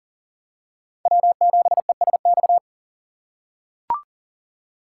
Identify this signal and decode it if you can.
Morse code, heard as a keyed tone, about 40 wpm, 700 Hz. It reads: W7ESX
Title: USA Call Signs - 5 Characters - 6x Character Spacing - 40wpm